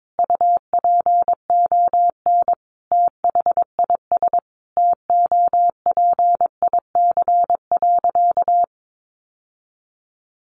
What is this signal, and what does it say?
Morse code, heard as a keyed tone, about 22 wpm, 700 Hz. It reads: UPON THIS TOPIC.
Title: Sentences from Top 1200 Words 22wpm